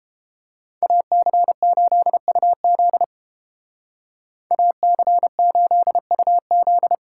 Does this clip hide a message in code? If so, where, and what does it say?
Yes: Morse code from 0.8 to 7.0 s, heard as a keyed tone, about 29 wpm, 700 Hz. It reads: AC8UZ AC8UZ